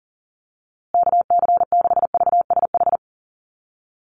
Morse code, heard as a keyed tone, about 40 wpm, 700 Hz. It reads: KC6VSH
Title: USA Call Signs Repeated 2x 40wpm